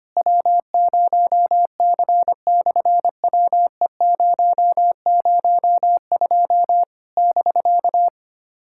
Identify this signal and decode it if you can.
Morse code, heard as a keyed tone, about 25 wpm, 700 Hz. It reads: W0C/WE002 <BK>
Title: SOTA QSO Practice 25wpm